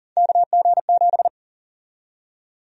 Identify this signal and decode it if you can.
Morse code, heard as a keyed tone, about 37 wpm, 700 Hz. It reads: KG7
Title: USA Call Sign Prefixes - 40wpm